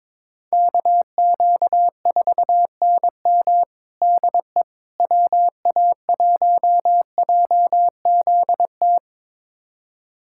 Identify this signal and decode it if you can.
Morse code, heard as a keyed tone, about 22 wpm, 700 Hz. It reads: KQ4NM DE WA1JZT
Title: USA Callsign DE Callsign 22wpm